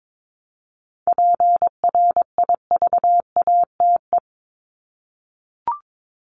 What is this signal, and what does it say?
Morse code, heard as a keyed tone, about 22 wpm, 700 Hz. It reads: PRIVATE
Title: Top 751-1000 Words 22wpm